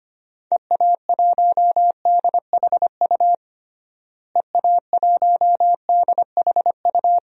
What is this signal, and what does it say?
Morse code, heard as a keyed tone, about 25 wpm, 700 Hz. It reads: EA1DHU EA1DHU
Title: DXCC-100 Call Signs - 6 Characters - Repeated 3x - 25wpm